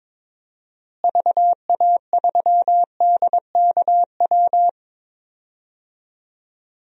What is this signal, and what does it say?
Morse code, heard as a keyed tone, about 22 wpm, 700 Hz. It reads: VA3DKW
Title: DXCC-100 Call Signs - 6 Characters - Repeated 3x - 22wpm